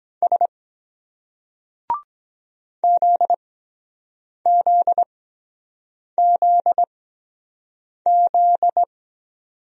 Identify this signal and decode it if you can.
Morse code, heard as a keyed tone, about 23 wpm, spaced as for 7 wpm, 700 Hz. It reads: S ZZZZ